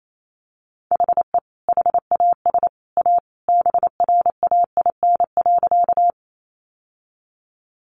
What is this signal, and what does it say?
Morse code, heard as a keyed tone, about 28 wpm, 700 Hz. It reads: HE HAS A BRAIN.
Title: Sentences from Top 1100 Words 28wpm